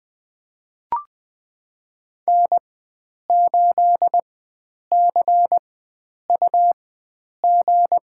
Morse code, heard as a keyed tone, about 20 wpm, spaced as for 9 wpm, 700 Hz. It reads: N8CUG
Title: USA Call Signs - 4x Character Spacing - 20wpm